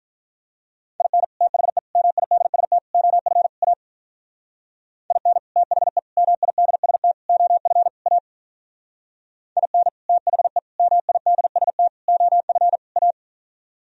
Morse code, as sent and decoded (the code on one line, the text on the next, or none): .. -. / - .... / -- .. -.. ... - / --- ..-. / .- / .. -. / - .... / -- .. -.. ... - / --- ..-. / .- / .. -. / - .... / -- .. -.. ... - / --- ..-. / .-
IN THE MIDST OF A IN THE MIDST OF A IN THE MIDST OF A